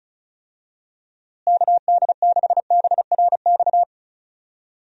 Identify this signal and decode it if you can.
Morse code, heard as a keyed tone, about 34 wpm, 700 Hz. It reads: KD6BRX
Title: USA Call Signs - 6 Characters - Repeated 3x - 35wpm